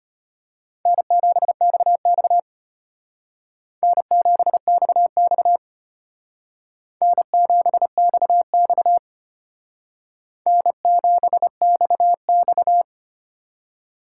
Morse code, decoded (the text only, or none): N7XX N7XX N7XX N7XX